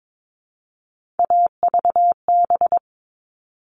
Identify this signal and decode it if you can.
Morse code, heard as a keyed tone, about 22 wpm, 700 Hz. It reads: AVB
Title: All Cut Numbers - 3-Digit 22wpm